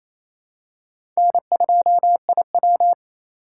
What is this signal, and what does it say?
Morse code, heard as a keyed tone, about 28 wpm, 700 Hz. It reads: N2IW
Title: USA Call Signs - 4 Characters - 3x Character Spacing - 28wpm